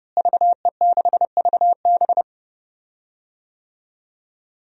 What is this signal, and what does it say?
Morse code, heard as a keyed tone, about 30 wpm, 700 Hz. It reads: VE6VB
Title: DXCC-100 Call Signs - 5 Characters - Repeated 3x - 30wpm